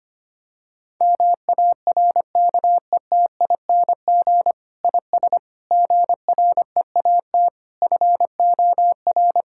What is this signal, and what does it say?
Morse code, heard as a keyed tone, about 25 wpm, 700 Hz. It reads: MARKETING IS GREAT FOR
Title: Sentences from Top 1000 Words - Review Part 5 - 25wpm